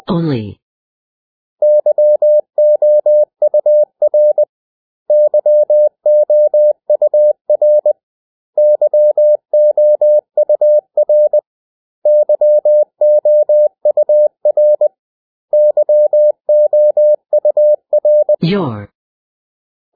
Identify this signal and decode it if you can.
Morse code, heard as a keyed tone, about 20 wpm, 600 Hz. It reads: YOUR YOUR YOUR YOUR YOUR
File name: cw / 100 Common English Words